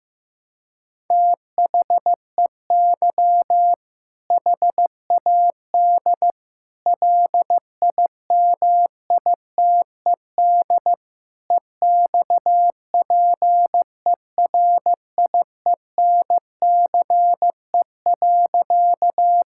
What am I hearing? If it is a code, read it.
Morse code, heard as a keyed tone, about 15 wpm, 700 Hz. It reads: THEY HAD LIMITED EXPERIENCE.
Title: Sentences from Top 1000 Words - Review Part 6 - 15wpm